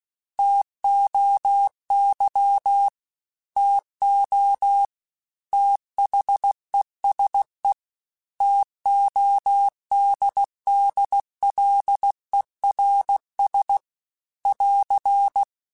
Morse code, heard as a keyed tone, about 16 wpm, 800 Hz. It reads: TOY TO THESE TODDLERS <AR>
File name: T Runs / T Runs mp3s / T Runs @15